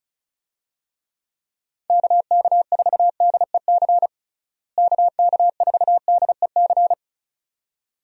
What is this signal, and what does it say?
Morse code, heard as a keyed tone, about 36 wpm, 700 Hz. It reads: KK4DEC KK4DEC
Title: USA Call Signs - 6 Characters - Repeated 2x - 35wpm